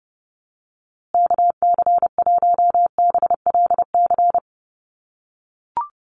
Morse code, heard as a keyed tone, about 30 wpm, 700 Hz. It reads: KC1BLC
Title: USA Call Signs Repeated 3x 30wpm